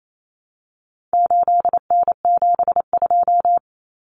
Morse code, heard as a keyed tone, about 28 wpm, 700 Hz. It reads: 8N72
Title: Common Cut Numbers - 4-Digit 28wpm